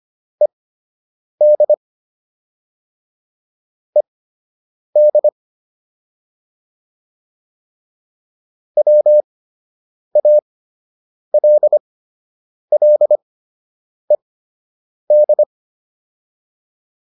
Morse code, heard as a keyed tone, about 25 wpm, spaced as for 8 wpm, 600 Hz. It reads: ED ED WALLED
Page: fundamental-practice-files